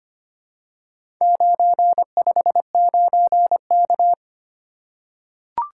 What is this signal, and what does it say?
Morse code, heard as a keyed tone, about 25 wpm, 700 Hz. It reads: 959K
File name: practice / 25wpm / Four Letter-Number Combinations - 25wpm